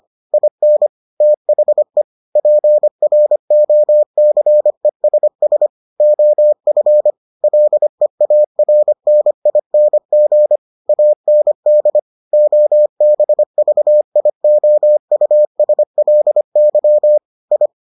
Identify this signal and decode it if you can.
Morse code, heard as a keyed tone, about 25 wpm, 600 Hz. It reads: IN THE PROCESS OF LEARNING AND OBVIOUSLY I